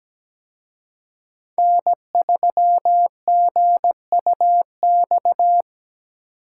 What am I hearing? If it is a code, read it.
Morse code, heard as a keyed tone, about 20 wpm, 700 Hz. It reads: N3GUX
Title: Speed Racing - USA Call Signs 17wpm